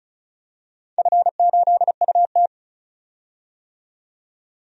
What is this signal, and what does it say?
Morse code, heard as a keyed tone, about 36 wpm, 700 Hz. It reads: F8UT
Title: DXCC-100 Call Signs - 4 Characters - 35wpm